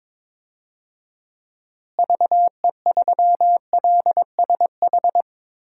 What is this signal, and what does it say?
Morse code, heard as a keyed tone, about 21 wpm, 700 Hz. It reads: VE3LSH
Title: DXCC-100 Call Signs - 6 Characters - Repeated 3x - 22wpm